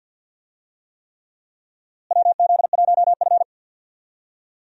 Morse code, heard as a keyed tone, about 53 wpm, 700 Hz. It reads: WZ1F
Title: USA Call Signs - 4 Characters - Repeated 3x - 50wpm